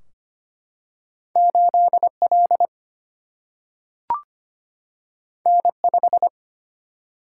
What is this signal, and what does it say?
Morse code, heard as a keyed tone, about 25 wpm, 700 Hz. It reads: 8L N5
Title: Single-Letter Single-Number 25wpm